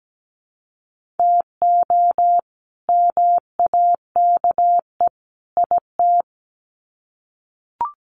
Morse code, heard as a keyed tone, about 17 wpm, 700 Hz. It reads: TO MAKE IT
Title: Sets of 3 Words 17wpm